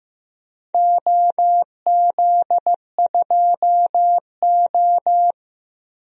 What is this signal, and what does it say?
Morse code, heard as a keyed tone, about 15 wpm, 700 Hz. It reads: OZ2O
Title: DXCC-100 Call Signs - 4 Characters - Repeated 2x - 15wpm